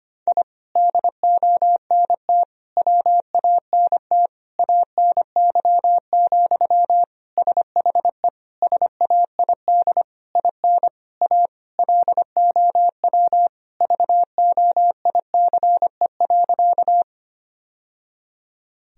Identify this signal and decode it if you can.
Morse code, heard as a keyed tone, about 25 wpm, 700 Hz. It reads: I DONT WANT ANY, SHE SAID IN A LOW VOICE.
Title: Sentences from Top 1000 Words - Review Part 1 - 25wpm